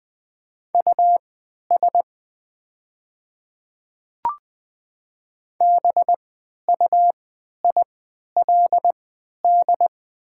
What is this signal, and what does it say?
Morse code, heard as a keyed tone, about 20 wpm, spaced as for 11 wpm, 700 Hz. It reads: US BUILD
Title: 3x Character Spacing - Top 300 Words - 20wpm